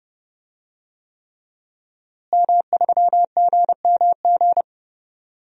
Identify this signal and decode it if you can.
Morse code, heard as a keyed tone, about 30 wpm, 700 Hz. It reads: M3GMG
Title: DXCC-100 Call Signs - 5 Characters - Repeated 3x - 30wpm